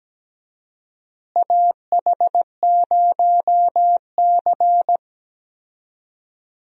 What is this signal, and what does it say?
Morse code, heard as a keyed tone, about 17 wpm, 700 Hz. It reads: AH0C
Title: USA Call Signs - 4 Characters - Repeated 2x - 17wpm